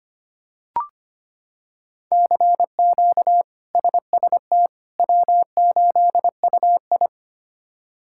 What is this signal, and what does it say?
Morse code, heard as a keyed tone, about 25 wpm, 700 Hz. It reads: CQ SST W8UI
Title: K1USN SST QSO Practice - 25wpm